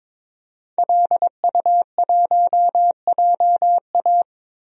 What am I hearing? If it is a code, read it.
Morse code, heard as a keyed tone, about 22 wpm, 700 Hz. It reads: LU1JA
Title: DXCC-100 Call Signs - 5 Characters - Repeated 3x - 22wpm